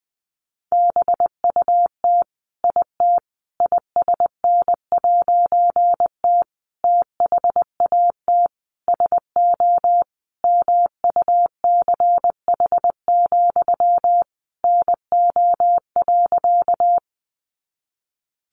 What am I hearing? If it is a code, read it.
Morse code, heard as a keyed tone, about 20 wpm, 700 Hz. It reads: BUT IT ISN'T THAT SO MUCH, NO.
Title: A Sound Of Thunder 20wpm